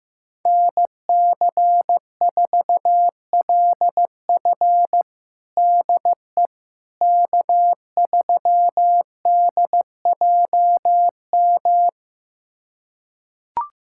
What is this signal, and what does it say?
Morse code, heard as a keyed tone, about 15 wpm, 700 Hz. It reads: NC4LF DE K3DJM
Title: USA Callsign DE Callsign - 5 Characters - 15wpm